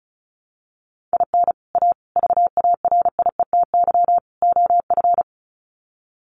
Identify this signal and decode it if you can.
Morse code, heard as a keyed tone, about 34 wpm, 700 Hz. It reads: IN A VARIETY OF
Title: Sets of 4 Words 35wpm